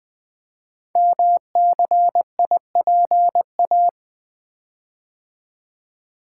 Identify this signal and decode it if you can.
Morse code, heard as a keyed tone, about 20 wpm, 700 Hz. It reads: MCIPA
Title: Code Groups - 5 Letters - 20wpm